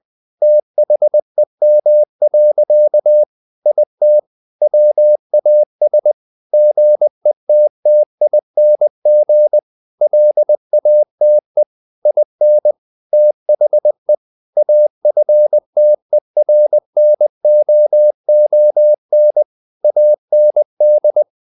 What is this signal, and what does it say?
Morse code, heard as a keyed tone, about 20 wpm, 600 Hz. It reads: THEM. IT WAS GETTING LATE IN THE AFTERNOON AND
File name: the old man and the sea part 2 20wpm